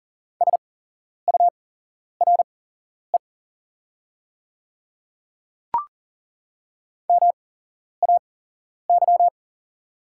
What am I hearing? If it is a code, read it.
Morse code, heard as a keyed tone, about 40 wpm, spaced as for 11 wpm, 700 Hz. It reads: SURE MAY
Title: 8x Character Spacing - Top 300 Words - 40wpm